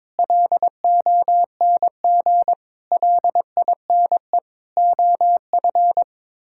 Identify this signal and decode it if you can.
Morse code, heard as a keyed tone, about 22 wpm, 700 Hz. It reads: LONG LINE OF
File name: practice / 22wpm / Sets of 8 Words - 22wpm